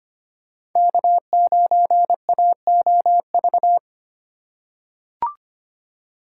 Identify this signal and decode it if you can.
Morse code, heard as a keyed tone, about 25 wpm, 700 Hz. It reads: K9AOV
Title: USA Call Signs - 5 Characters - 3x Character Spacing - 25wpm